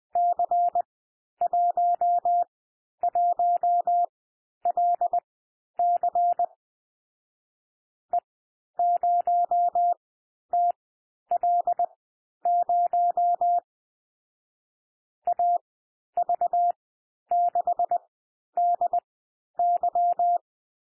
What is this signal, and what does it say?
Morse code, heard as a keyed tone, about 20 wpm, spaced as for 11 wpm, 700 Hz. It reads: C11LC E0TL0 AV6DY
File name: cwmp3 / M1 SOUND MIXED 20 wpm